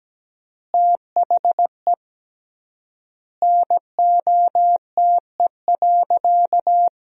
Morse code, heard as a keyed tone, about 17 wpm, 700 Hz. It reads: THE NOTE.
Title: 3x Word-Spacing - Sentences from Top 500 Words - 17wpm